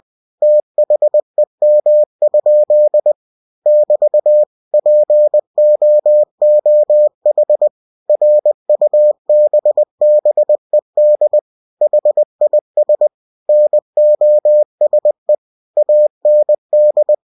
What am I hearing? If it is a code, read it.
Morse code, heard as a keyed tone, about 20 wpm, 600 Hz. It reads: THEM? <BT> POOH RUBBED HIS NOSE AND
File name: cw-books / winnie-the-pooh / winnie-20wpm-page-0061